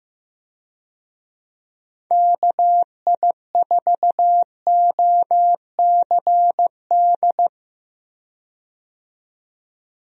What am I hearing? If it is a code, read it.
Morse code, heard as a keyed tone, about 15 wpm, 700 Hz. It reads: KI4OCD
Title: USA Call Signs - 6 Characters - 15wpm